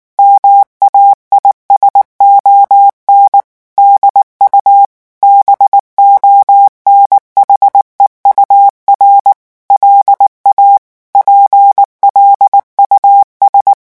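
Morse code, heard as a keyed tone, about 19 wpm, 800 Hz. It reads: MAISON DU BONHEUR LA PLUS
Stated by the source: French Morse revision 8 drill of random five-letter groups at 18 words per minute.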